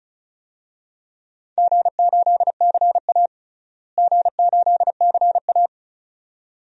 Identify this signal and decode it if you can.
Morse code, heard as a keyed tone, about 36 wpm, 700 Hz. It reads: G8CA G8CA
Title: DXCC-100 Call Signs - 4 Characters - Repeated 2x - 35wpm